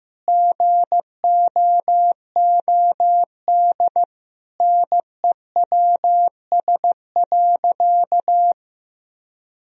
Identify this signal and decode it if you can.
Morse code, heard as a keyed tone, about 15 wpm, 700 Hz. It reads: GOOD NEWS.
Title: Sentences from Top 1000 Words - Review Part 3 - 15wpm